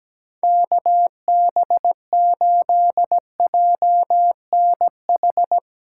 Morse code, heard as a keyed tone, about 17 wpm, 700 Hz. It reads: KB8JNH
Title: USA Callsign DE Callsign - 6 Characters - 17wpm